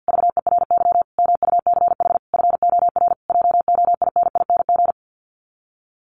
Morse code, heard as a keyed tone, about 50 wpm, 700 Hz. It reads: VERY MUCH FOR JOINING
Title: Sets of 5 Words Encore 50wpm